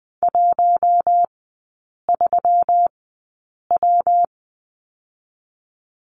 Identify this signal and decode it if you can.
Morse code, heard as a keyed tone, about 20 wpm, 700 Hz. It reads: 1 3 W
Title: 121 Lesson 20e - Number 3 - Three Characters 20wpm